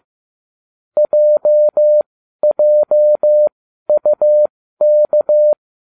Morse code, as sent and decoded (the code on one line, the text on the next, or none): .--- .--- ..- -.-
JJUK